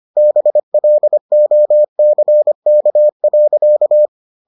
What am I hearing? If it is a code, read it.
Morse code, heard as a keyed tone, about 25 wpm, 600 Hz. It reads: BLOCK.